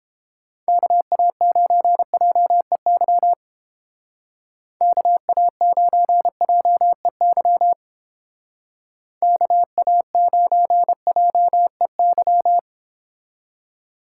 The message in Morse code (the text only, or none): KA9JEY KA9JEY KA9JEY